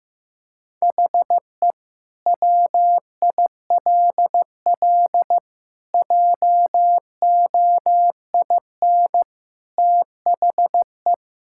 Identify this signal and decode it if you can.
Morse code, heard as a keyed tone, about 15 wpm, 700 Hz. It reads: HE WILL JOIN THE
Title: Sentences from Top 1000 Words - Review Part 6 - 15wpm